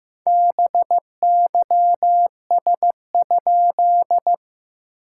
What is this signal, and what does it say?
Morse code, heard as a keyed tone, about 15 wpm, 690 Hz. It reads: BYS?
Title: SOTA QSO Practice 15wpm